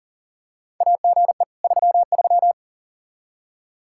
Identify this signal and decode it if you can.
Morse code, heard as a keyed tone, about 39 wpm, 700 Hz. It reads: AGE 33
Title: Practice QSO - Exchange 3 - More Info - 40wpm